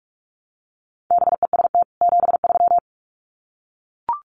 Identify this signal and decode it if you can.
Morse code, heard as a keyed tone, about 44 wpm, 700 Hz. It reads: BEST 73
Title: Common QSO Phrases 45wpm